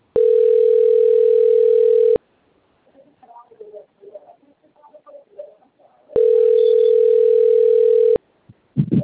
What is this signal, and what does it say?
Ringback tone (North America), 440 + 480 Hz, with speech.
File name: rg-1600-unknown- - - .28529